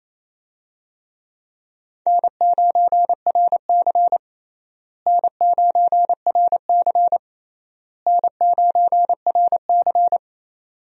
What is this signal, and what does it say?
Morse code, heard as a keyed tone, about 28 wpm, 700 Hz. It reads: N9RC N9RC N9RC
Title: USA Call Signs - 4 Characters - Repeated 3x - 28wpm